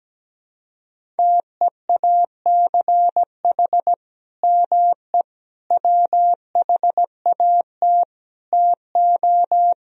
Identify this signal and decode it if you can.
Morse code, heard as a keyed tone, about 17 wpm, 700 Hz. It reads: TEACH ME WHAT TO
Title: Sentences from Top 500 Words and Abbreviations 17wpm